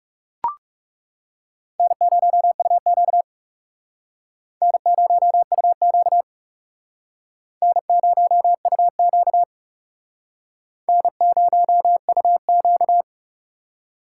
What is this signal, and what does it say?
Morse code, heard as a keyed tone, about 37 wpm, 700 Hz. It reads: N0UQ N0UQ N0UQ N0UQ
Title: Speed Racing - USA 1x2 Call Signs 30wpm